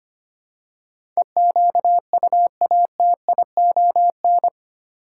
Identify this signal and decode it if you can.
Morse code, heard as a keyed tone, about 25 wpm, 700 Hz. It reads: EQUATION
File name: 8 Letter Words - First 3 Letters Given - 25wpm